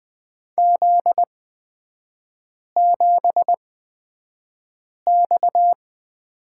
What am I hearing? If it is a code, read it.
Morse code, heard as a keyed tone, about 20 wpm, spaced as for 5 wpm, 700 Hz. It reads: Z7X
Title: 243 Lesson 39e - Letter z - Three Characters 20wpm